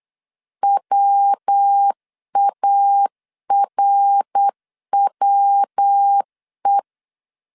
Morse code, heard as a keyed tone, about 8 wpm, 800 Hz. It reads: WARWE
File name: CW / DET / 1 3 1500C S C 8wpm 4214